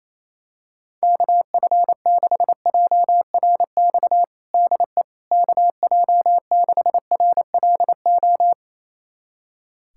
Morse code, heard as a keyed tone, about 28 wpm, 700 Hz. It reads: KF6JRX DE KJ6RLO
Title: USA Callsign DE Callsign 28wpm